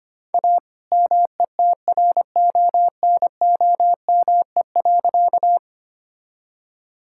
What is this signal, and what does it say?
Morse code, heard as a keyed tone, about 25 wpm, 700 Hz. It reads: A METRONOME.
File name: practice / 25wpm / All-Age Appropriate Jokes - 25wpm